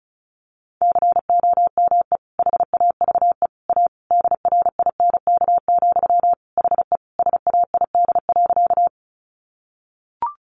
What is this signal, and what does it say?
Morse code, heard as a keyed tone, about 34 wpm, 700 Hz. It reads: COME HAVE A DRINK, HE SAID.
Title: Sentences from Top 1500 Words 35wpm